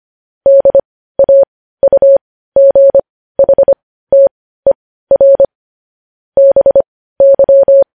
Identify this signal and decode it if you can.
Morse code, heard as a keyed tone, about 25 wpm, spaced as for 15 wpm, 550 Hz. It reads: DAUGHTER BY
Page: news-headlines-in-morse-code-at-15-wpm